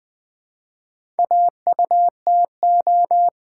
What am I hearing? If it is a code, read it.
Morse code, heard as a keyed tone, about 20 wpm, 690 Hz. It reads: AUTO
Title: Build A Word - 2 Letters At A Time - 4-Letter Words - 20wpm